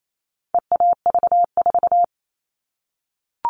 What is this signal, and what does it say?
Morse code, heard as a keyed tone, about 28 wpm, 700 Hz. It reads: EAV4
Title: All Cut Numbers - 4-Digit 28wpm